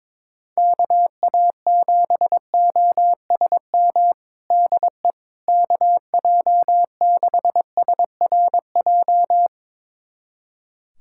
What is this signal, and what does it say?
Morse code, heard as a keyed tone, about 21 wpm, 700 Hz. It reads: KA7OSM DE KJ6SRJ
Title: USA Callsign DE Callsign 22wpm